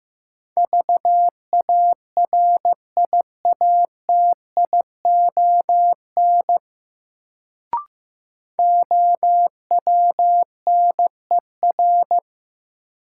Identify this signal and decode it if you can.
Morse code, heard as a keyed tone, about 16 wpm, 700 Hz. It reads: VARIATION OWNER
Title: Sentences from Top 1600 Words - New Word Drill Alt - 15wpm